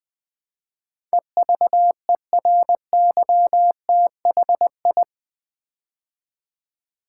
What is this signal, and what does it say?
Morse code, heard as a keyed tone, about 20 wpm, 700 Hz. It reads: EVERYTHI
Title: Build A Word - 4 Letters At A Time - 10-Letter Words - 20wpm